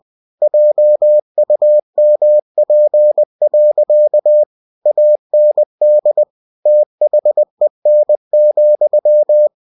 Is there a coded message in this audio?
Yes: Morse code from 0.4 to 9.5 s, heard as a keyed tone, about 20 wpm, 600 Hz. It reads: JUMP. AND THEN,